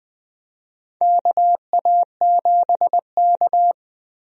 Morse code, heard as a keyed tone, about 20 wpm, 700 Hz. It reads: KA7K
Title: USA Call Signs - 4 Characters - 20wpm